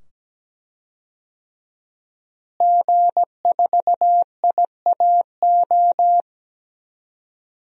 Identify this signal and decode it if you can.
Morse code, heard as a keyed tone, about 17 wpm, 700 Hz. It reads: G4IAO
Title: DXCC-100 Call Signs - 5 Characters - Repeated 2x - 17wpm